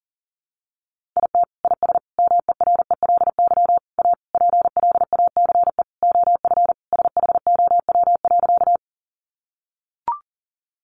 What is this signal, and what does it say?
Morse code, heard as a keyed tone, about 40 wpm, 700 Hz. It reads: IT IS MERELY A PLACE OF SHOW.
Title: Sentences from Top 1400 Words 40wpm